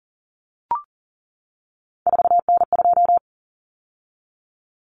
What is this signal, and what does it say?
Morse code, heard as a keyed tone, about 40 wpm, 700 Hz. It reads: 4N2
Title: Common Cut Numbers - 3-Digit 40wpm